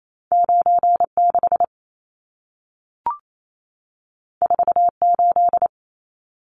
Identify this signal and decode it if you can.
Morse code, heard as a keyed tone, about 28 wpm, 700 Hz. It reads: 96 48
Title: Numbers - 2-Digit 28wpm